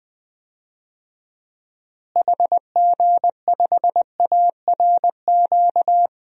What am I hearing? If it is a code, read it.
Morse code, heard as a keyed tone, about 20 wpm, 700 Hz. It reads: HG5ARQ
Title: DXCC-100 Call Signs - 6 Characters - Repeated 2x - 20wpm